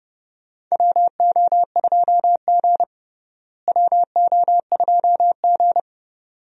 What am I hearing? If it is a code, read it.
Morse code, heard as a keyed tone, about 31 wpm, 700 Hz. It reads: WO2G WO2G
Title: USA Call Signs - 4 Characters - Repeated 2x - 30wpm